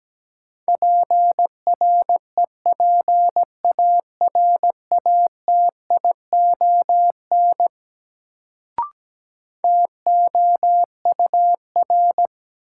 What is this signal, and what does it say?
Morse code, heard as a keyed tone, about 17 wpm, 700 Hz. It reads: PREPARATION TOUR
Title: Sentences from Top 1400 Words - New Word Drill - 17wpm